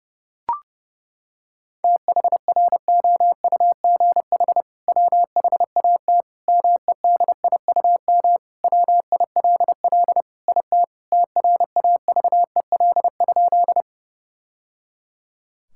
Morse code, heard as a keyed tone, about 30 wpm, 700 Hz. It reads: THROUGH WHAT MEDIUM WILL IT TRAVEL?
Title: Sentences from Top 1000 Words - Review Part 6 - 30wpm